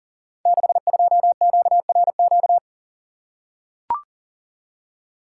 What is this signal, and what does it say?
Morse code, heard as a keyed tone, about 42 wpm, 690 Hz. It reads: 62QRQ
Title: Code Groups - 5 Letter:Numbers - 40wpm